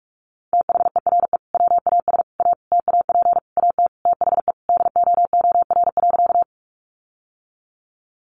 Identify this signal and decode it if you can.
Morse code, heard as a keyed tone, about 44 wpm, 700 Hz. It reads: THERE WAS A TAP AT THE DOOR.
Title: Sentences from Top 1400 Words 45wpm